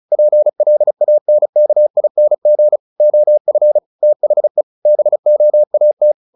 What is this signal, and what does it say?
Morse code, heard as a keyed tone, about 34 wpm, 600 Hz. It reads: PLANKING OF THE BOAT
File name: the old man and the sea / the old man and the sea 1 35wpm